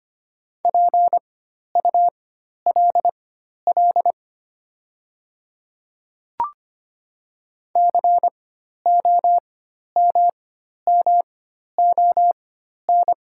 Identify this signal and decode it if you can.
Morse code, heard as a keyed tone, about 25 wpm, spaced as for 12 wpm, 690 Hz. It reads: PULL COMMON
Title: 4x Character Spacing - Top 500 Words - 25wpm